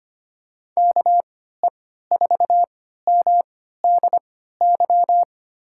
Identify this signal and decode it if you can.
Morse code, heard as a keyed tone, about 25 wpm, spaced as for 14 wpm, 700 Hz. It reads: KE4MDY
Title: USA Call Signs - 3x Character Spacing - 25wpm